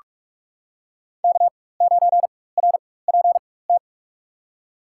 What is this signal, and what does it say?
Morse code, heard as a keyed tone, about 47 wpm, spaced as for 21 wpm, 700 Hz. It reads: K9RPT
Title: USA Call Signs - 5 Characters - 4x Character Spacing - 45wpm